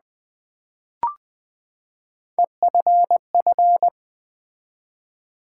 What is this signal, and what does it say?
Morse code, heard as a keyed tone, about 20 wpm, 700 Hz. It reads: EFF
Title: Build A Word - 3 Letters At A Time - 6-Letter Words - 20wpm